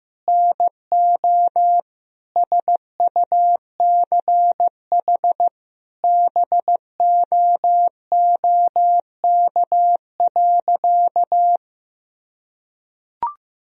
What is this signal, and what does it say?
Morse code, heard as a keyed tone, about 15 wpm, 700 Hz. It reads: NO SUCH BOOK.
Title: Sentences from Top 1000 Words - Review Part 1 - 15wpm